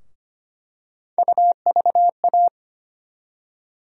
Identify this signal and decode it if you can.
Morse code, heard as a keyed tone, about 25 wpm, 700 Hz. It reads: UVA
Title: USA Call Sign Postfixes - 3 Letters 25wpm